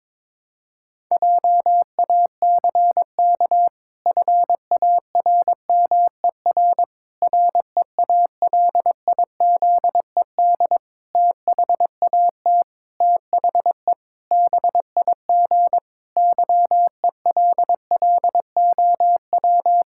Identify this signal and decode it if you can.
Morse code, heard as a keyed tone, about 22 wpm, 700 Hz. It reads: JACK FARMER REALIZED THAT THE BIG YELLOW
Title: Pangram Challenge v2 - 22wpm